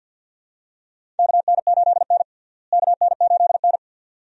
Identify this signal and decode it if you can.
Morse code, heard as a keyed tone, about 50 wpm, 700 Hz. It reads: KN8N KN8N
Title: USA Call Signs - 4 Characters - Repeated 2x - 50wpm